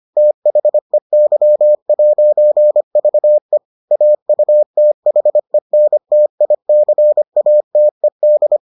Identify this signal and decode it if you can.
Morse code, heard as a keyed tone, about 25 wpm, 600 Hz. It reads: THEY'VE AUTHENTICATED